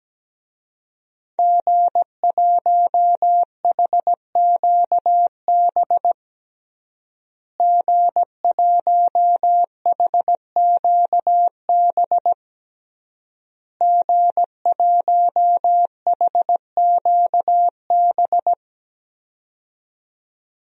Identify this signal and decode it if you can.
Morse code, heard as a keyed tone, about 17 wpm, 700 Hz. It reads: G1HQB G1HQB G1HQB
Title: DXCC-100 Call Signs - 5 Characters - Repeated 3x - 17wpm